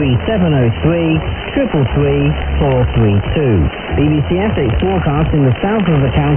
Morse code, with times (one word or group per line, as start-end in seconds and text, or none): none